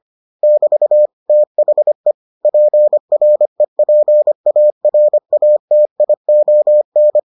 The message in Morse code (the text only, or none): <BT> THE PREPARATION